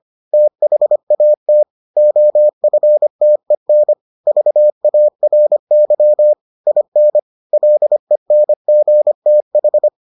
Morse code, heard as a keyed tone, about 25 wpm, 600 Hz. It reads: THAT OFTEN VARY IN LENGTH